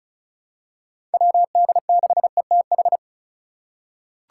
Morse code, heard as a keyed tone, about 36 wpm, 700 Hz. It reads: WD6ETH
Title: USA Call Signs - 6 Characters - 4x Character Spacing - 35wpm